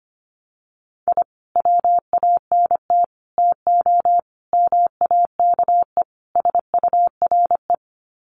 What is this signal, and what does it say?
Morse code, heard as a keyed tone, about 25 wpm, 700 Hz. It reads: I WANT TO MAKE SURE
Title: Sets of 5 Words Encore2 25wpm